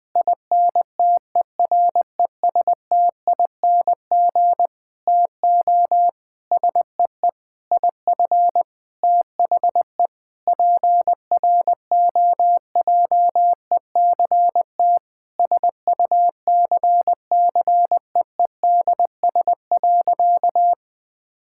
Morse code, heard as a keyed tone, about 19 wpm, 700 Hz. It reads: INTERESTING TO SEE IF THE PROJECT SUCCEEDS.
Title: Sentences with 8-Word Phrases - Spoken Beginning - 20wpm